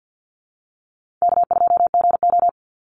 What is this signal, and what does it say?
Morse code, heard as a keyed tone, about 50 wpm, 700 Hz. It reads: K2GO
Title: Call Signs Repeated 2x - 50wpm